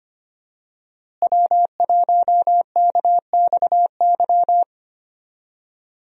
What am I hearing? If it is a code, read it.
Morse code, heard as a keyed tone, about 25 wpm, 700 Hz. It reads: W1KXY
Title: USA Call Signs - 5 Characters - Repeated 3x - 25wpm